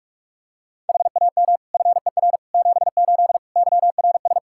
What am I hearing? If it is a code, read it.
Morse code, heard as a keyed tone, about 44 wpm, 700 Hz. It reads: HAM FER 78 YRS
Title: Practice QSO - Exchange 3 - More Info - 45wpm